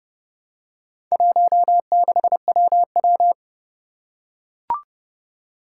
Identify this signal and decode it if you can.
Morse code, heard as a keyed tone, about 30 wpm, 700 Hz. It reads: 16WW
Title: Four Challenging Letter-Number Combinations 30wpm